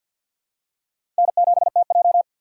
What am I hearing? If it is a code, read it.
Morse code, heard as a keyed tone, about 52 wpm, 700 Hz. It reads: N7TJ
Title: USA Call Signs - 4 Characters - 2x Character Spacing - 50wpm